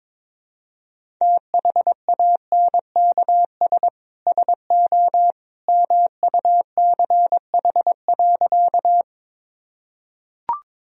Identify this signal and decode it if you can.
Morse code, heard as a keyed tone, about 21 wpm, 700 Hz. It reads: THANKS SO MUCH.
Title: Sentences from Top 1000 Words - Review Part 3 - 22wpm